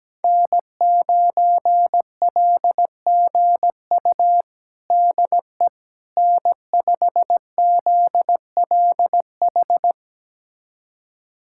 Morse code, heard as a keyed tone, about 17 wpm, 700 Hz. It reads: N9LGU DE N5ZLH
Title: USA Callsign DE Callsign - 5 Characters - 17wpm